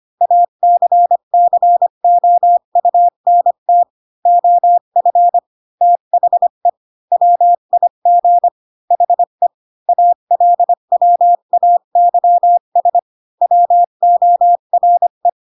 Morse code, heard as a keyed tone, about 25 wpm, 700 Hz. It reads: ACCOUNT OF THE WIG HE ALWAYS WORE
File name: Pinocchio / Pinocchio ch2 20 25 ik1hge